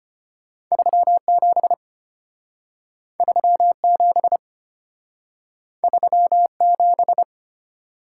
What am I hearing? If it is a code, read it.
Morse code, heard as a keyed tone, about 33 wpm, 690 Hz. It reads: 37 37 37